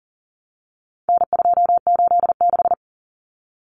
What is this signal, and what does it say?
Morse code, heard as a keyed tone, about 40 wpm, 700 Hz. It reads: N286
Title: Common Cut Numbers - 4-Digit 40wpm